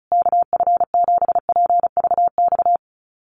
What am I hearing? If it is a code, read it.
Morse code, heard as a keyed tone, about 34 wpm, 700 Hz. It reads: KF7PVX
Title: USA Call Signs Repeated 3x 35wpm